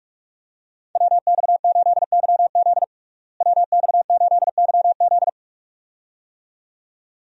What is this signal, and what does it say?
Morse code, heard as a keyed tone, about 43 wpm, 700 Hz. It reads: WX8YZ WX8YZ
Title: USA Call Signs - 5 Characters - Repeated 2x - 45wpm